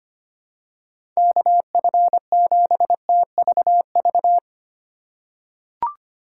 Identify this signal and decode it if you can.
Morse code, heard as a keyed tone, about 25 wpm, 700 Hz. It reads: KF7TVV
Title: USA Call Signs - 6 Characters - 8x Character Spacing - 25wpm